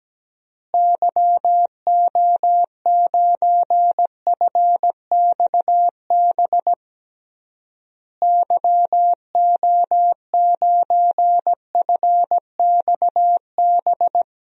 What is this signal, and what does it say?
Morse code, heard as a keyed tone, about 17 wpm, 700 Hz. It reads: YO9FXB YO9FXB
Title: DXCC-100 Call Signs - 6 Characters - Repeated 3x - 17wpm